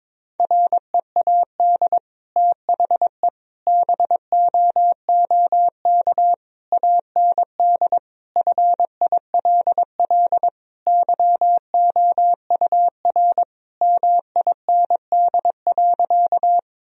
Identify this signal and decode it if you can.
Morse code, heard as a keyed tone, about 22 wpm, 700 Hz. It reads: READ THE BOOK AND FILL YOUR MIND.
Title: Sentences from Top 1000 Words - Review Part 3 - 22wpm